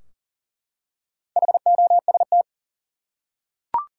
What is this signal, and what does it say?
Morse code, heard as a keyed tone, about 40 wpm, 700 Hz. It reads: HOST
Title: Four Letter Words - 40wpm